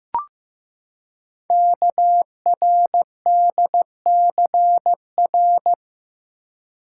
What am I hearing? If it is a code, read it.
Morse code, heard as a keyed tone, about 15 wpm, 700 Hz. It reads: KRDCR
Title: Code Groups - 5 Letters - 15wpm